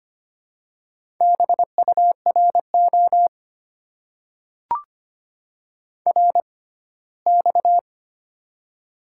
Morse code, heard as a keyed tone, about 25 wpm, spaced as for 17 wpm, 700 Hz. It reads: BURO RX
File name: Top 90 QSO Elements - 6x Character Spacing - 25wpm